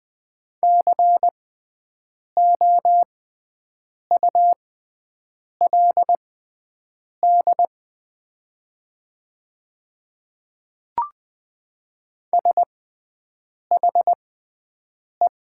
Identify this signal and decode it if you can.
Morse code, heard as a keyed tone, about 20 wpm, spaced as for 7 wpm, 700 Hz. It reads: COULD SHE
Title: 6x Character Spacing - Top 200 Words - 20wpm